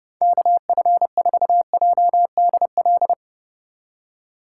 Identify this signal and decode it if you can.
Morse code, heard as a keyed tone, about 38 wpm, 700 Hz. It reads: KF4JDL